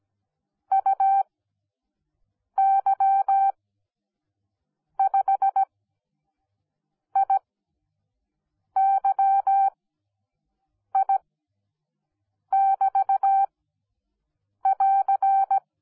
Morse code, heard as a keyed tone, about 17 wpm, spaced as for 6 wpm, 780 Hz. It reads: UY5IYI<BT><AR>